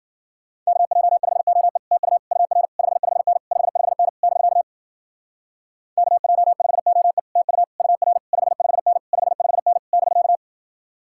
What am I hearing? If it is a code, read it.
Morse code, heard as a keyed tone, about 52 wpm, 700 Hz. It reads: KJ5OE TU UR 55N 55N <BK> KJ5OE TU UR 55N 55N <BK>